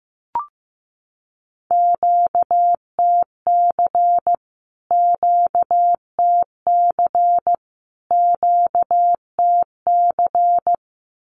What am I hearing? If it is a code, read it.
Morse code, heard as a keyed tone, about 15 wpm, 700 Hz. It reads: QTC QTC QTC
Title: Q-Codes Repeated 3x 15wpm